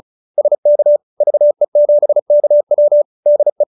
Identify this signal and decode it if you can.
Morse code, heard as a keyed tone, about 34 wpm, 590 Hz. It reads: SK VE7KW DE
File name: QSO314 35